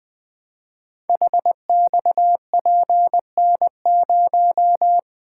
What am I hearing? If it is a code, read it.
Morse code, heard as a keyed tone, about 20 wpm, 700 Hz. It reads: HXPN0
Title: Code Groups - 5 Letter:Numbers - 20wpm